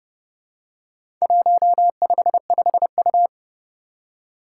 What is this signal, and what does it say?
Morse code, heard as a keyed tone, about 30 wpm, 700 Hz. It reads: 155U
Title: Four Challenging Letter-Number Combinations 30wpm